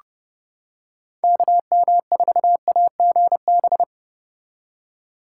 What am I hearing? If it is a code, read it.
Morse code, heard as a keyed tone, about 30 wpm, 700 Hz. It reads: KM4AGB